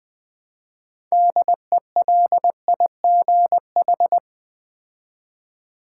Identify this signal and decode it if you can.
Morse code, heard as a keyed tone, about 20 wpm, 700 Hz. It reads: DELIGH
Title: Build A Word - 3 Letters At A Time - 9-Letter Words - 20wpm